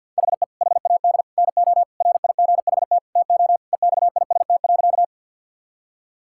Morse code, heard as a keyed tone, about 51 wpm, 690 Hz. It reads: HE HAD NO RIGHT TO EXIST.
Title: Sentences from Top 1000 Words - Review Part 5 - 50wpm